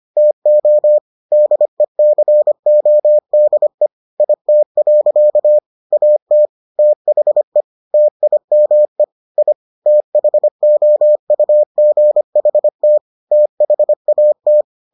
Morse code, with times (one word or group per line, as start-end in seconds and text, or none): 0.2-1.0 TO
1.3-3.9 DECODE
4.2-5.6 IT.
5.9-6.5 AT
6.8-7.6 THE
7.9-9.0 TIME
9.4-9.5 I
9.9-13.0 THOUGHT
13.3-14.6 THAT